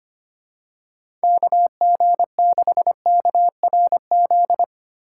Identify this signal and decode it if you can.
Morse code, heard as a keyed tone, about 25 wpm, 700 Hz. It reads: KG6KRZ
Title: USA Call Signs - 6 Characters - 8x Character Spacing - 25wpm